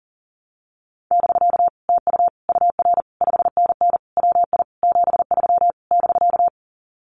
Morse code, heard as a keyed tone, about 40 wpm, 700 Hz. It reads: <BK> TU UR 5NN WI 73 <BK>
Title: Common QSO Phrases Encore 40wpm